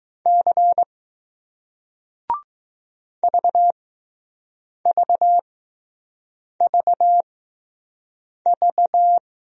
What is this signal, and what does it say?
Morse code, heard as a keyed tone, about 20 wpm, spaced as for 6 wpm, 700 Hz. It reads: C VVVV